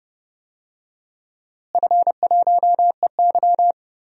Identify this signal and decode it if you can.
Morse code, heard as a keyed tone, about 30 wpm, 700 Hz. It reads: F1EY
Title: DXCC-100 Call Signs - 4 Characters - Repeated 3x - 30wpm